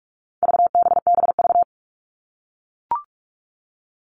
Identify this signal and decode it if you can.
Morse code, heard as a keyed tone, about 44 wpm, 700 Hz. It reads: VBBV
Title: All Cut Numbers - 4-Digit 45wpm